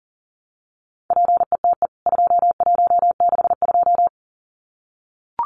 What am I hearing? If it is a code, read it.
Morse code, heard as a keyed tone, about 37 wpm, 700 Hz. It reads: PETE 2162
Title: CWT Exchange 40wpm